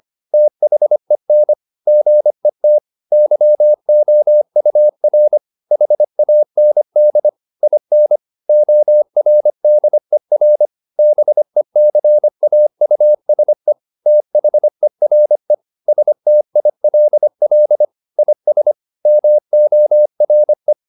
Morse code, heard as a keyed tone, about 25 wpm, 600 Hz. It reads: THEN GET YOUR HAND IN ORDER BECAUSE THERE STILL IS MORE